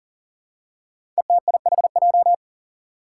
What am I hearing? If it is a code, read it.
Morse code, heard as a keyed tone, about 40 wpm, 700 Hz. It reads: ETIHJ
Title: Five Challenging Letter-Number Combinations 40wpm